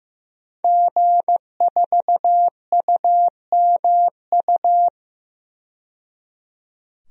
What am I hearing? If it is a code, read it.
Morse code, heard as a keyed tone, about 15 wpm, 700 Hz. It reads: G4UMU
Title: DXCC-100 Call Signs - 5 Characters - 15wpm